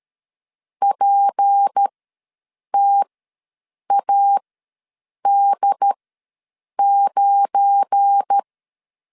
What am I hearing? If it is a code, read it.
Morse code, heard as a keyed tone, about 13 wpm, spaced as for 7 wpm, 800 Hz. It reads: PTAD9
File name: LEZIONI / 10 SIGNS / 10D COMA ws 12wpm 3618